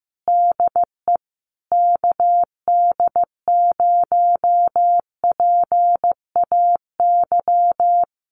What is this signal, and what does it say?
Morse code, heard as a keyed tone, about 15 wpm, 700 Hz. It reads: DE KD0PAY
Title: Common QSO Phrases Encore 15wpm